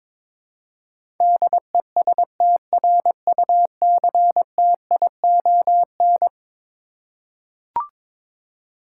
Morse code, heard as a keyed tone, about 22 wpm, 700 Hz. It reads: DESTRUCTION
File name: practice / 22wpm / Eleven Letter Words - 22wpm